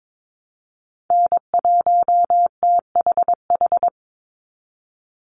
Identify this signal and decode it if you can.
Morse code, heard as a keyed tone, about 22 wpm, 700 Hz. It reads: N1THH
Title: USA Call Signs 22wpm